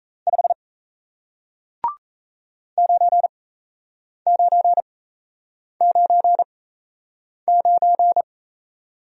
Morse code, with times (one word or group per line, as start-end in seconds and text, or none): 0.3-0.5 5
2.8-8.2 9999